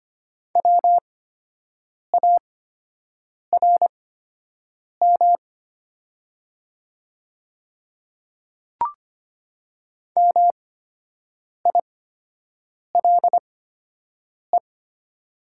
Morse code, heard as a keyed tone, about 25 wpm, spaced as for 7 wpm, 700 Hz. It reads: WARM MILE
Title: 8x Character Spacing - Top 500 Words - 25wpm